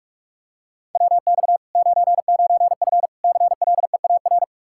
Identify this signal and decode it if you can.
Morse code, heard as a keyed tone, about 46 wpm, 700 Hz. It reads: WX 99F CLEAR
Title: Practice QSO - Exchange 2 - Info - 45wpm